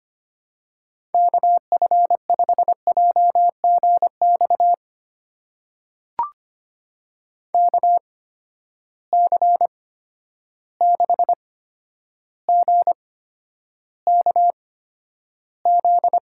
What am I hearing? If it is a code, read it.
Morse code, heard as a keyed tone, about 25 wpm, spaced as for 11 wpm, 700 Hz. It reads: KF5JGX KC6GKZ
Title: USA Call Signs - 6 Characters - 8x Character Spacing - 25wpm